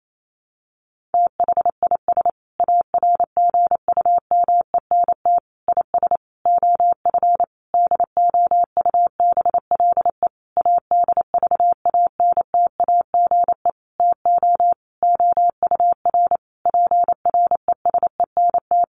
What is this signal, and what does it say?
Morse code, heard as a keyed tone, about 28 wpm, 700 Hz. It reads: THIS ARGUMENT IS OF DOUBLE ADVANTAGE TO OUR PRESENT
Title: Sentences from Top 1400 Words 28wpm